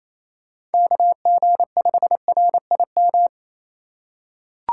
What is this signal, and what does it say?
Morse code, heard as a keyed tone, about 28 wpm, 700 Hz. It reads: KG5RIM
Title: USA Call Signs - 6 Characters - 4x Character Spacing - 28wpm